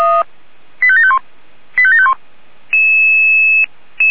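Five-tone selective call (ZVEI), heard as quick runs of five tones, tones of about 70 ms. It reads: 85751 85751
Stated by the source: ZVEI five-tone selective call 85741, received on an SDR.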